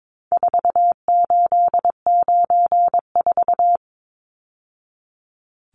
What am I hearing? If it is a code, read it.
Morse code, heard as a keyed tone, about 22 wpm, 700 Hz. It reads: 4894
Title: Numbers - 4-Digit 22wpm